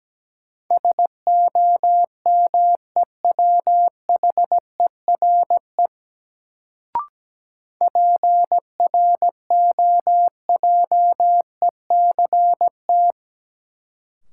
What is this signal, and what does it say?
Morse code, heard as a keyed tone, about 17 wpm, 700 Hz. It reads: SOMEWHERE PROJECT
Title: Sentences from Top 1400 Words - New Word Drill - 17wpm